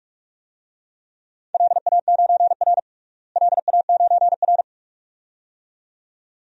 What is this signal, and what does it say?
Morse code, heard as a keyed tone, about 46 wpm, 700 Hz. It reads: LA9R LA9R
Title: DXCC-100 Call Signs - 4 Characters - Repeated 2x - 45wpm